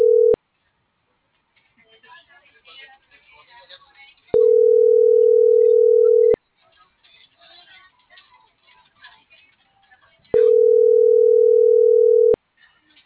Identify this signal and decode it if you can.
Ringback tone (North America), 440 + 480 Hz.